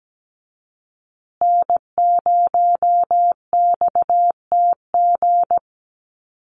Morse code, heard as a keyed tone, about 17 wpm, 700 Hz. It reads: N0XTG
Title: USA Call Signs Repeated 3x 17wpm